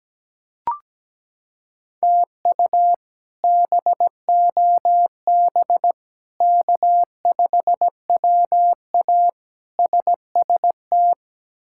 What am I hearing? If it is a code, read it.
Morse code, heard as a keyed tone, about 17 wpm, 700 Hz. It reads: TU BOB K5WA SST
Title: K1USN SST QSO Practice - 17wpm